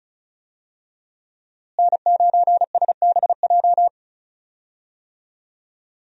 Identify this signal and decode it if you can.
Morse code, heard as a keyed tone, about 34 wpm, 700 Hz. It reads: N9SBJ
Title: USA Call Signs - 5 Characters - Repeated 3x - 35wpm